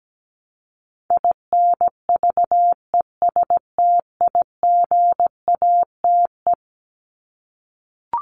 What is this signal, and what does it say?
Morse code, heard as a keyed tone, about 17 wpm, 700 Hz. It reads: INVESTIGATE
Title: Top 1751-2000 Words 17wpm